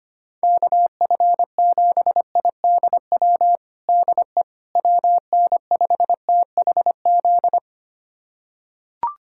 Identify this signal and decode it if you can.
Morse code, heard as a keyed tone, about 25 wpm, 700 Hz. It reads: KF7IDW DE WN5THZ
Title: USA Callsign DE Callsign - 6 Characters - 25wpm